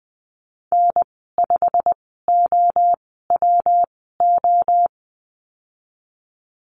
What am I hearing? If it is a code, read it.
Morse code, heard as a keyed tone, about 20 wpm, spaced as for 15 wpm, 700 Hz. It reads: N5OWO
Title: 128 Lesson 21g - Number 6 - Call Signs 20wpm